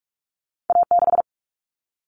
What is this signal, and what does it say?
Morse code, heard as a keyed tone, about 44 wpm, 700 Hz. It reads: A6
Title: Common Cut Numbers - 2-Digit 45wpm